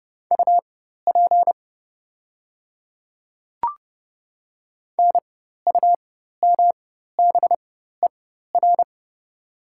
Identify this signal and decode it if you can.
Morse code, heard as a keyed tone, about 30 wpm, spaced as for 14 wpm, 700 Hz. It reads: UP NUMBER
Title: 4x Character Spacing - Top 100 Words - 30wpm